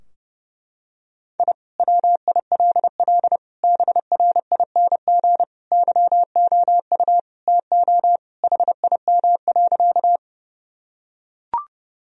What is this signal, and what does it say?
Morse code, heard as a keyed tone, about 30 wpm, 700 Hz. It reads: I WILL BRING YOU TO HIM.
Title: Sentences from Top 1000 Words - Review Part 3 - 30wpm